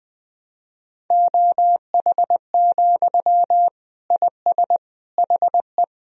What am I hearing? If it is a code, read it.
Morse code, heard as a keyed tone, about 20 wpm, 700 Hz. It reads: OH, IS HE
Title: Sentences from Top 1000 Words - Review Part 1 - 20wpm